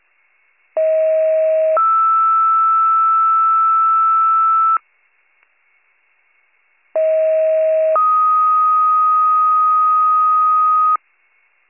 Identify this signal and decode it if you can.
Two-tone page, heard as a one-second tone, then a three-second tone, about 640 then 1320 Hz, about 640 then 1250 Hz. It is heard from 0.8 to 11.0 s.